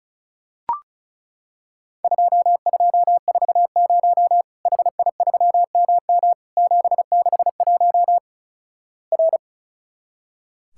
Morse code, heard as a keyed tone, about 36 wpm, 700 Hz. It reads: 2240 HI3MM 761 R
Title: Worked All Europe DX Contest - QTC Exchange - 35wpm